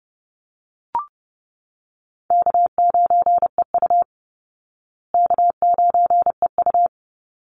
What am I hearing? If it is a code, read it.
Morse code, heard as a keyed tone, about 30 wpm, 700 Hz. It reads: K9EU K9EU
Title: Easy Call Signs Repeated 2x - 30wpm